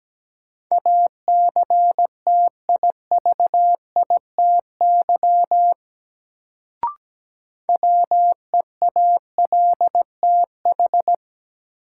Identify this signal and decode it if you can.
Morse code, heard as a keyed tone, about 17 wpm, 700 Hz. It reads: ACTIVITY WEALTH
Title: Sentences from Top 1000 Words - Review Part 4 - New Word Drill - 17wpm